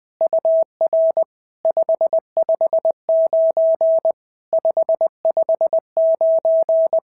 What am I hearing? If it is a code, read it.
Morse code, heard as a keyed tone, about 20 wpm, 650 Hz. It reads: UR 559 559
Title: SOTA QSO Practice 20wpm